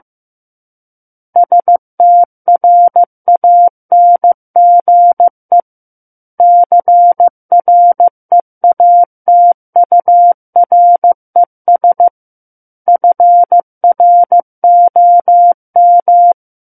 Morse code, heard as a keyed tone, about 15 wpm, 700 Hz. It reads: STRANGE CREATURES FROM
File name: Book One / 06-Chapter 6 - 15 WPM